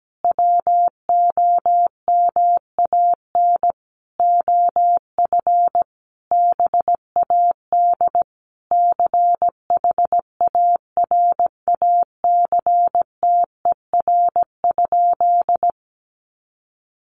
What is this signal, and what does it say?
Morse code, heard as a keyed tone, about 17 wpm, 700 Hz. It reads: WOMAN OF BAD CHARACTER?
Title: Sentences from Top 1200 Words 17wpm